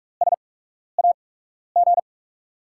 Morse code, heard as a keyed tone, about 43 wpm, spaced as for 12 wpm, 700 Hz. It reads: SAG
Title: USA Call Signs - 5 Characters - 8x Character Spacing - 45wpm